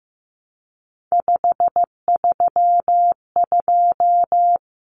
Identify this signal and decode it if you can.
Morse code, heard as a keyed tone, about 15 wpm, 700 Hz. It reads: 532
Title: Common Cut Numbers - 3-Digit 15wpm